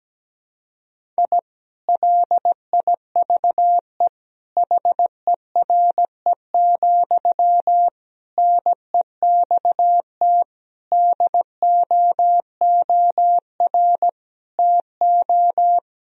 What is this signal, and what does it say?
Morse code, heard as a keyed tone, about 17 wpm, 700 Hz. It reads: I LIVE HERE, NEXT DOOR TO
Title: Sentences from Top 1000 Words - Review Part 1 - 17wpm